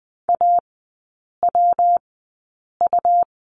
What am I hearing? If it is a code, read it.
Morse code, heard as a keyed tone, about 20 wpm, 700 Hz. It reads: A W U
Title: 114 Lesson 19e - Letter w - Three Characters 20wpm